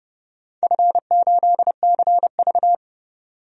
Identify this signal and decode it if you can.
Morse code, heard as a keyed tone, about 31 wpm, 700 Hz. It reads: F8CV
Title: DXCC-100 Call Signs - 4 Characters - 30wpm